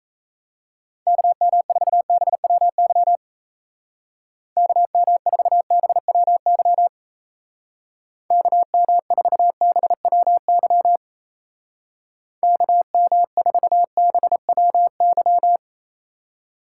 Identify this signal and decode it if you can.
Morse code, heard as a keyed tone, about 34 wpm, 700 Hz. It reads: KM4BWY KM4BWY KM4BWY KM4BWY
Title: Speed Racing - USA 2x3 Call Signs 28wpm